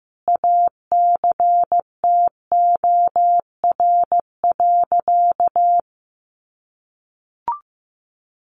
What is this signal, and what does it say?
Morse code, heard as a keyed tone, about 15 wpm, 700 Hz. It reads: ACTOR.
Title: Sentences from Top 1500 Words 15wpm